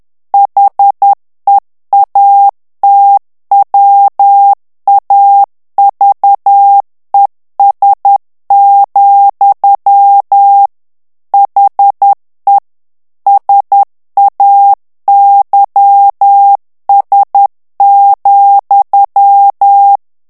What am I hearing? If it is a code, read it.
Morse code, heard as a keyed tone, about 10 wpm, 800 Hz. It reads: HEATWAVES, HE SAYS,
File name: morse mp3 / ANGLAIS texte clair 2 vitesse 10 mots minute